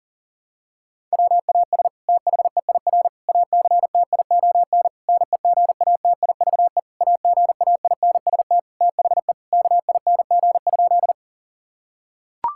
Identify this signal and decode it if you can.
Morse code, heard as a keyed tone, about 37 wpm, 700 Hz. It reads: WAS THEIR ACTION NEGATIVE AGAINST THE KING?
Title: Sentences from Top 1000 Words - Review Part 6 - 40wpm